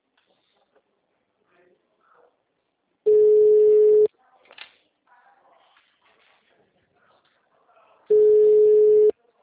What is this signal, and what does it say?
Ringback tone (Europe, 425 Hz), with speech.